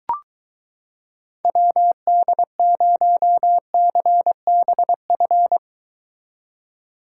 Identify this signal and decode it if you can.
Morse code, heard as a keyed tone, about 22 wpm, 700 Hz. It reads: WD0CBF